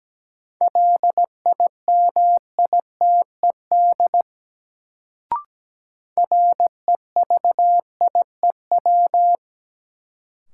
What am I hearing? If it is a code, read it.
Morse code, heard as a keyed tone, about 17 wpm, 700 Hz. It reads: LIMITED REVIEW
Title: Sentences from Top 900 Words - New Word Drill - 17wpm